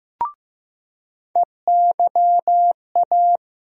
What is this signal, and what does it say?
Morse code, heard as a keyed tone, about 15 wpm, 700 Hz. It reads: EYA
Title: USA Call Sign Postfixes - 3 Letters 15wpm